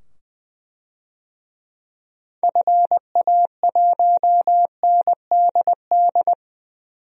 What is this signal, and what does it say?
Morse code, heard as a keyed tone, about 20 wpm, 700 Hz. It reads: FA1NDD
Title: DXCC-100 Call Signs - 6 Characters - 20wpm